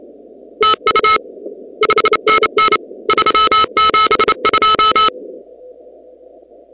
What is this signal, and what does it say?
Morse code, heard as a keyed tone, about 29 wpm, 1300 Hz, with noise. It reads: TU 5NN 372
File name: contests / 2019-cw-wpx-kt5j-k5pi- -2330 0-37